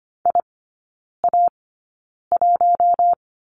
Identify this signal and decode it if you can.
Morse code, heard as a keyed tone, about 25 wpm, spaced as for 9 wpm, 690 Hz. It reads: IA1
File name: Lesson 14e - Number 2 - Three Characters 25wpm